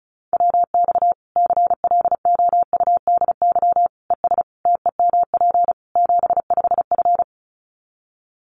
Morse code, heard as a keyed tone, about 34 wpm, 700 Hz. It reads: WX CLOUDY ES TEMP 75F
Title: Common QSO Phrases 35wpm